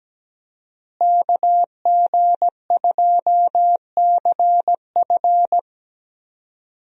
Morse code, heard as a keyed tone, about 17 wpm, 700 Hz. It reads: KG2CF
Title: USA Call Signs - 5 Characters - Repeated 3x - 17wpm